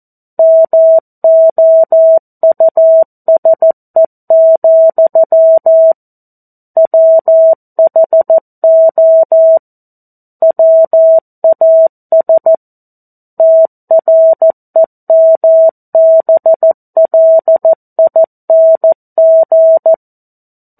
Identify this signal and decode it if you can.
Morse code, heard as a keyed tone, about 14 wpm, 650 Hz. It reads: MOUSE, WHO WAS TREMBLING
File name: 02-Chapter-02 14 WPM